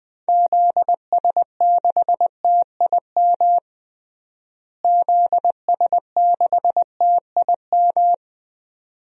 Morse code, heard as a keyed tone, about 21 wpm, 700 Hz. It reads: ZS6TIM ZS6TIM
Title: DXCC-100 Call Signs - 6 Characters - Repeated 2x - 20wpm